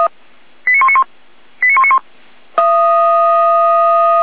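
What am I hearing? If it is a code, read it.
Five-tone selective call (ZVEI), heard as quick runs of five tones, tones of about 70 ms. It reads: 89191 89191
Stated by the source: ZVEI five-tone selective call received by SDR.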